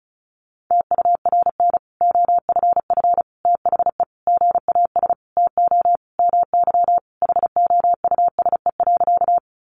Morse code, heard as a keyed tone, about 34 wpm, 700 Hz. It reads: TURN OFF THE GAS TO MY HOUSE.
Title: Sentences from Top 1100 Words 35wpm